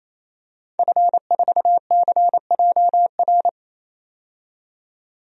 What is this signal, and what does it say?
Morse code, heard as a keyed tone, about 28 wpm, 700 Hz. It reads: F4CJR
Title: DXCC-100 Call Signs - 5 Characters - 28wpm